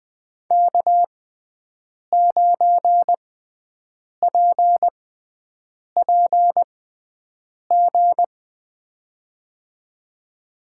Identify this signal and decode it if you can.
Morse code, heard as a keyed tone, about 20 wpm, spaced as for 7 wpm, 700 Hz. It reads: K9PPG
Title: USA Call Signs - 5 Characters - 6x Character Spacing - 20wpm